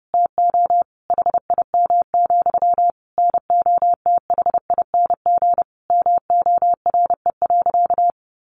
Morse code, heard as a keyed tone, about 30 wpm, 700 Hz. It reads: TO HIM, NOTHING MORE.
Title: Sentences from Top 1300 Words 30wpm